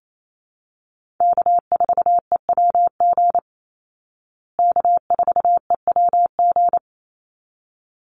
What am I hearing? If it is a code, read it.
Morse code, heard as a keyed tone, about 28 wpm, 700 Hz. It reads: K4EWG K4EWG
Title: Call Signs Repeated 3x - 28wpm